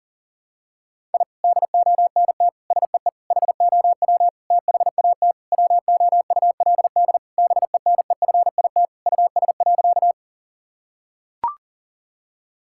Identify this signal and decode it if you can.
Morse code, heard as a keyed tone, about 40 wpm, 700 Hz. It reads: I DONT SEE HOW THAT WOULD BENEFIT US.
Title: Sentences from Top 1000 Words - Review Part 6 - 40wpm